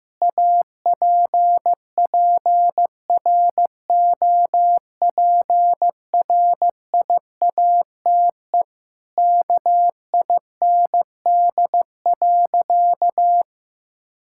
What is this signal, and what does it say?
Morse code, heard as a keyed tone, about 15 wpm, 700 Hz. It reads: APPROPRIATE KIND.
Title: Sentences from Top 1000 Words - Review Part 5 - 15wpm